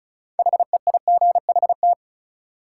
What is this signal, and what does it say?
Morse code, heard as a keyed tone, about 36 wpm, 700 Hz. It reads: HEIGHT
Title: Sentences from Top 1600 Words - New Word Drill - 35wpm